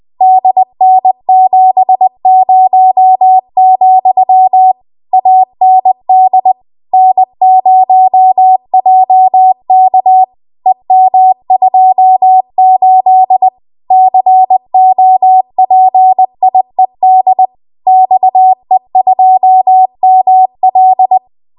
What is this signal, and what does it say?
Morse code, heard as a keyed tone, about 20 wpm, 750 Hz. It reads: DN70, AND N0JK EM28 COPIED XE2ML
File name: Archive / 20 WPM / 20WPM